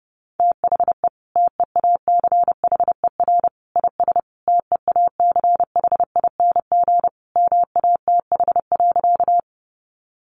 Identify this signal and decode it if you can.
Morse code, heard as a keyed tone, about 30 wpm, 700 Hz. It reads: THE TEACHER IS TEACHING MATH.
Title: Easy Phrases 30wpm